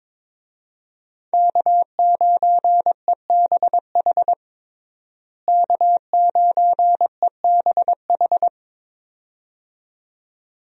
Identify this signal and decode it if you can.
Morse code, heard as a keyed tone, about 22 wpm, 700 Hz. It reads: K9EBH K9EBH
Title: USA Call Signs - 5 Characters - Repeated 2x - 22wpm